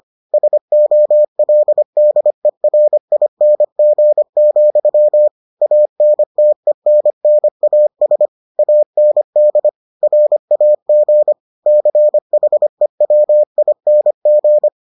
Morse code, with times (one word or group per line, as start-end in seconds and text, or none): 0.3-5.3 SOLDERING,
5.6-8.3 ANTENNAS
8.6-9.7 AND
10.0-11.3 RAG
11.7-14.7 CHEWING